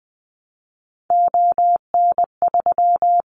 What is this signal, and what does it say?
Morse code, heard as a keyed tone, about 20 wpm, 700 Hz. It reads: ON3
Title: Callsign Prefixes for Top 100 DXCC - 20wpm